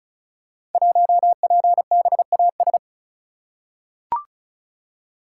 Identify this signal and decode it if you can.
Morse code, heard as a keyed tone, about 36 wpm, 700 Hz. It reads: 1PBAS
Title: Code Groups - 5 Letter:Numbers - 35wpm